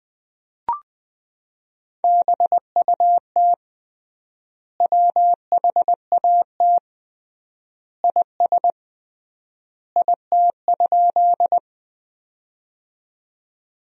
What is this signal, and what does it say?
Morse code, heard as a keyed tone, about 20 wpm, 700 Hz. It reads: BUT WHAT IS IT?
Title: 3x Word-Spacing - Sentences from Top 100 Words - 20wpm